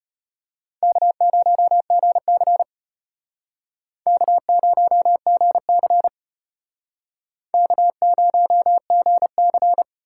Morse code, heard as a keyed tone, about 33 wpm, 700 Hz. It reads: K0GC K0GC K0GC